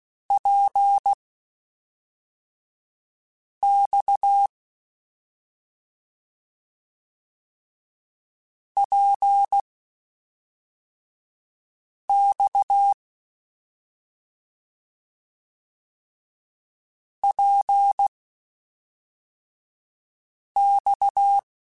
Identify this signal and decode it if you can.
Morse code, heard as a keyed tone, about 16 wpm, spaced as for 3 wpm, 800 Hz. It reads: PX PX PX
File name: Short Runs / Short Runs-mp3s / Balanced @5